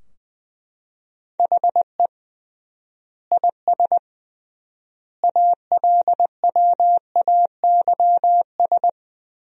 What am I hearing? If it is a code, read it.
Morse code, heard as a keyed tone, about 20 wpm, 700 Hz. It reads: HE IS ALWAYS
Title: 3x Word-Spacing - Sentences from Top 300 Words - 20wpm